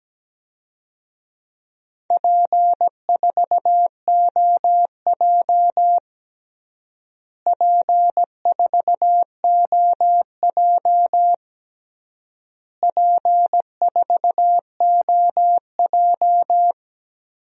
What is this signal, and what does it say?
Morse code, heard as a keyed tone, about 17 wpm, 700 Hz. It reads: P4OJ P4OJ P4OJ
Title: DXCC-100 Call Signs - 4 Characters - Repeated 3x - 17wpm